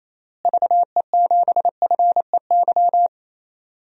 Morse code, heard as a keyed tone, about 27 wpm, 700 Hz. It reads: VE7FEY
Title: DXCC-100 Call Signs - 6 Characters - Repeated 3x - 28wpm